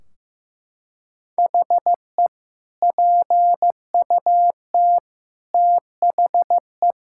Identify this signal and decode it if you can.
Morse code, heard as a keyed tone, about 15 wpm, 700 Hz. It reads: HE PUT THE
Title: Sentences from Top 1600 Words 15wpm